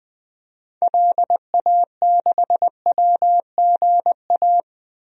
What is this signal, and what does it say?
Morse code, heard as a keyed tone, about 20 wpm, 700 Hz. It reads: LA6WGA
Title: DXCC-100 Call Signs - 6 Characters - Repeated 2x - 20wpm